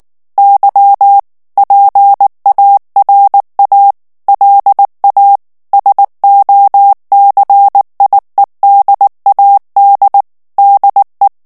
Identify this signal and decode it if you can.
Morse code, heard as a keyed tone, about 19 wpm, 800 Hz. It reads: Y PARA LA SOCIEDAD DE
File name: morse mp3 / ESPAGNOL TEXTE EN CLAIR 2 VITESSE 18 MOTS MINUTES